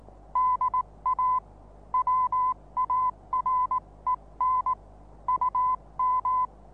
Morse code, heard as a keyed tone, about 19 wpm, 1010 Hz. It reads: DA WAREN UM
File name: repos / Morsekurs / DJ4UF Morsen26